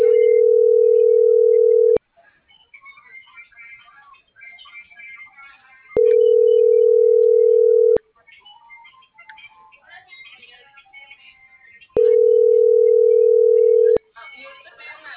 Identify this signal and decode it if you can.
Ringback tone (North America), 440 + 480 Hz.